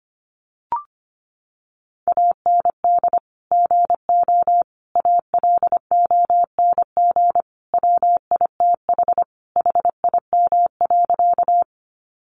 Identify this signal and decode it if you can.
Morse code, heard as a keyed tone, about 25 wpm, 700 Hz. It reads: AND GO ALONG WITH HIM.
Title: Sentences from Top 700 Words 25wpm